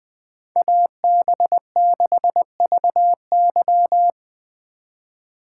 Morse code, heard as a keyed tone, about 24 wpm, 700 Hz. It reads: AB6VY
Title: Speed Racing - USA 2x2 Call Signs 20wpm